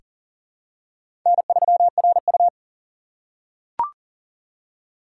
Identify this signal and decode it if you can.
Morse code, heard as a keyed tone, about 40 wpm, 700 Hz. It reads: N3RU
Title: USA Call Signs - 4 Characters - 2x Character Spacing - 40wpm